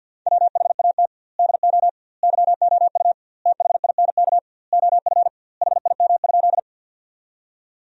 Morse code, heard as a keyed tone, about 50 wpm, 700 Hz. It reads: WHAT DO YOU THINK OF HIM?
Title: 2x Word-Spacing - Sentences from Top 200 Words - 50wpm